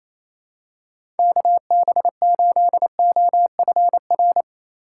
Morse code, heard as a keyed tone, about 27 wpm, 700 Hz. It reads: KB8OFR
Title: USA Call Signs - 6 Characters - 4x Character Spacing - 28wpm